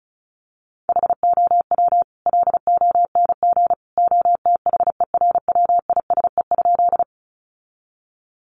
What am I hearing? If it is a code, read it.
Morse code, heard as a keyed tone, about 34 wpm, 700 Hz. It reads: HOW LONG OTHERWISE?
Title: Sentences from Top 1200 Words 35wpm